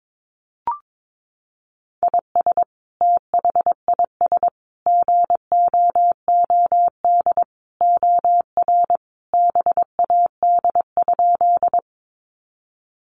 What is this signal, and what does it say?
Morse code, heard as a keyed tone, about 22 wpm, 700 Hz. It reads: IS THIS GOOD OR BAD?
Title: Sentences from Top 700 Words 22wpm